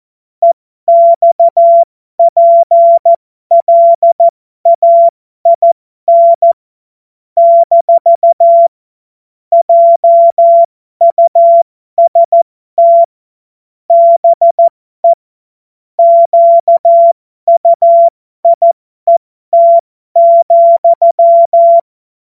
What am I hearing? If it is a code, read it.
Morse code, heard as a keyed tone, about 14 wpm, spaced as for 12 wpm, 680 Hz. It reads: EXPLAIN - JUST BE QUIET,
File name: Morsekurs / Witze